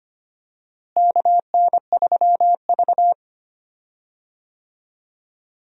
Morse code, heard as a keyed tone, about 25 wpm, 700 Hz. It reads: KN3V
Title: USA Call Signs - 4 Characters - Repeated 3x - 25wpm